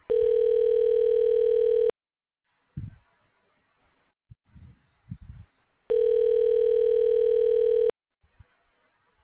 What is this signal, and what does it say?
Ringback tone (North America), 440 + 480 Hz.